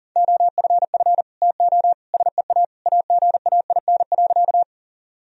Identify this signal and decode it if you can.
Morse code, heard as a keyed tone, about 40 wpm, 700 Hz. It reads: OFF TO SEA AGAIN.
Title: Sentences from Top 1000 Words - Review Part 1 - 40wpm